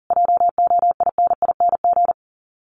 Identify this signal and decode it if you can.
Morse code, heard as a keyed tone, about 40 wpm, 700 Hz. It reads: JOINING
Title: Sets of 5 Words Encore2 40wpm